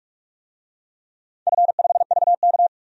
Morse code, heard as a keyed tone, about 47 wpm, 700 Hz. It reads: F5VK
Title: DXCC-100 Call Signs - 4 Characters - 45wpm